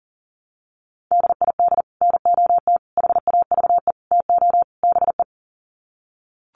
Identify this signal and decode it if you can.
Morse code, heard as a keyed tone, about 40 wpm, 700 Hz. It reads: DID NOT HAVE TO BE
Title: Sets of 5 Words Encore2 40wpm